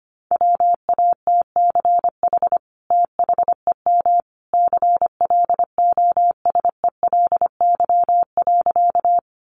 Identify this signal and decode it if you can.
Morse code, heard as a keyed tone, about 25 wpm, 700 Hz. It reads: WATCH THEM CLOSELY.
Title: Sentences from Top 1400 Words 25wpm